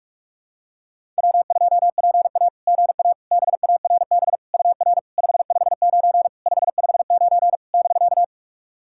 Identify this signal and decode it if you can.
Morse code, heard as a keyed tone, about 44 wpm, 690 Hz. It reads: W2PA GA BARB UR 559 559 <BK>
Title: SOTA QSO Practice 45wpm